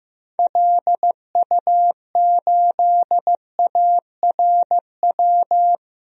Morse code, heard as a keyed tone, about 15 wpm, 700 Hz. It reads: LU8ARW
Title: DXCC-100 Call Signs - 6 Characters - 15wpm